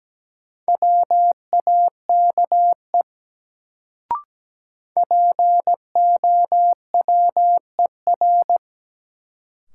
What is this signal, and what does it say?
Morse code, heard as a keyed tone, about 17 wpm, 700 Hz. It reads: WAKE POWER
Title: Sets of 4 Words - New Word Drill - 17wpm